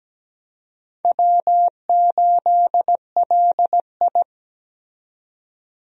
Morse code, heard as a keyed tone, about 17 wpm, 700 Hz. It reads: W8LI
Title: USA Call Signs - 4 Characters - 17wpm